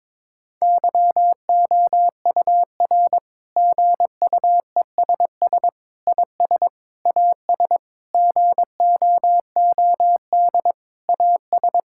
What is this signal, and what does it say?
Morse code, heard as a keyed tone, about 22 wpm, 700 Hz. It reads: YOUR GUESS IS AS GOOD AS
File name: practice / 22wpm / Sets of 7 Words - 22wpm